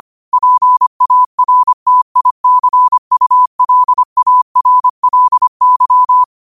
Morse code, heard as a keyed tone, about 25 wpm, 1000 Hz. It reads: PARTICULARLY
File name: qotdfiles / 25wpm